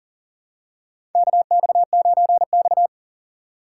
Morse code, heard as a keyed tone, about 39 wpm, 690 Hz. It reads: KX9X
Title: USA Call Signs - 6x Character Spacing - 40wpm